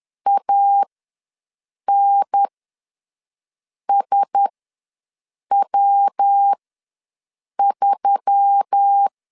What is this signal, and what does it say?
Morse code, heard as a keyed tone, about 10 wpm, spaced as for 6 wpm, 800 Hz. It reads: ANSW3
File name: LEZIONI / 6 PKX0 / 6 PK ws 10wpm 3745